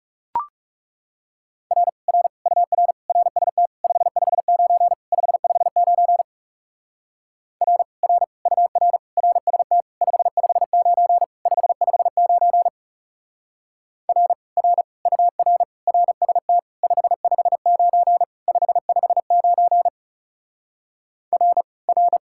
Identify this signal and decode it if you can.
Morse code, heard as a keyed tone, about 36 wpm, 700 Hz. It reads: R R UR RST 559 559 R R UR RST 559 559 R R UR RST 559 559 R R
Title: Speed-Racing Common QSO Phrases - 30wpm